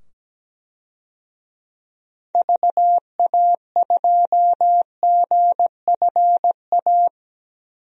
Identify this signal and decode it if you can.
Morse code, heard as a keyed tone, about 17 wpm, 700 Hz. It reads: VA2GFA
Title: DXCC-100 Call Signs - 6 Characters - 17wpm